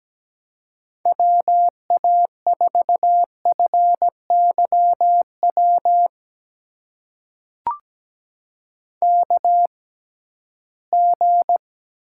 Morse code, heard as a keyed tone, about 17 wpm, spaced as for 13 wpm, 700 Hz. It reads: WA4FYW KG
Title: USA Call Signs - 6 Characters - 6x Character Spacing - 17wpm